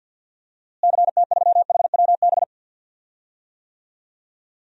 Morse code, heard as a keyed tone, about 49 wpm, 700 Hz. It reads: CT3HWB
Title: DXCC-100 Call Signs - 6 Characters - Repeated 3x - 50wpm